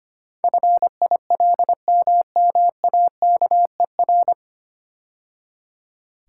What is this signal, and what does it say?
Morse code, heard as a keyed tone, about 25 wpm, 700 Hz. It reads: FILMMAKER
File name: practice / 25wpm / 9 Letter Words - First 3 Letters Given - 25wpm